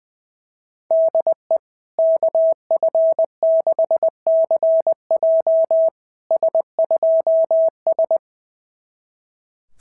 Morse code, heard as a keyed tone, about 21 wpm, 650 Hz. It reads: DE KF6CJ S2S
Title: SOTA QSO Practice 20wpm